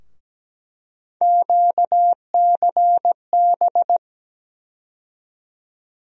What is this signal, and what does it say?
Morse code, heard as a keyed tone, about 17 wpm, 700 Hz. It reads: QCB
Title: USA Call Sign Postfixes - 3 Letters 17wpm